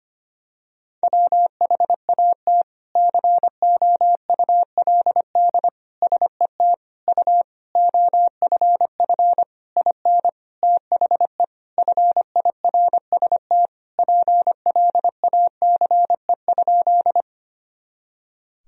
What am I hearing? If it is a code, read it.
Morse code, heard as a keyed tone, about 25 wpm, 700 Hz. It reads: WHAT COULD SET U OFF IN THE FIRST PLACE?
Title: Sentences from Top 200 Words and Abbreviations 25wpm